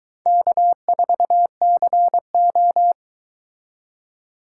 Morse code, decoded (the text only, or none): K4CO